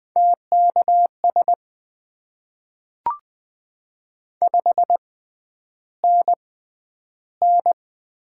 Morse code, heard as a keyed tone, about 20 wpm, spaced as for 10 wpm, 700 Hz. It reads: TKS 5NN
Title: Top 90 QSO Elements - 6x Character Spacing - 20wpm